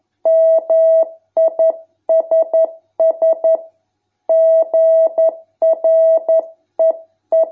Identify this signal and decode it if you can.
Morse code, heard as a keyed tone, about 11 wpm, 650 Hz. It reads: MISS GREE